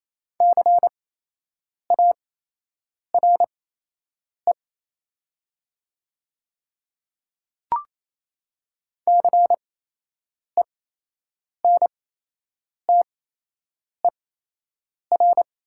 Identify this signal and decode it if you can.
Morse code, heard as a keyed tone, about 28 wpm, spaced as for 8 wpm, 700 Hz. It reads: CARE CENTER
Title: 8x Character Spacing - Top 500 Words - 28wpm